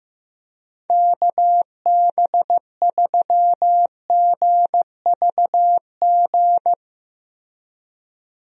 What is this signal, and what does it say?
Morse code, heard as a keyed tone, about 15 wpm, 700 Hz. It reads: KB3GVG
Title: USA Call Signs - 6 Characters - Repeated 3x - 15wpm